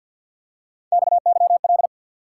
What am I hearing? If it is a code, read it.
Morse code, heard as a keyed tone, about 53 wpm, 700 Hz. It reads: XYL
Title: Top 90 QSO Elements 50wpm